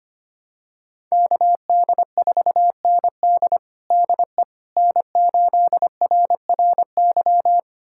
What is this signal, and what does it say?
Morse code, heard as a keyed tone, about 25 wpm, 700 Hz. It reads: KD4ND DE N8RRY
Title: USA Callsign DE Callsign - 5 Characters - 25wpm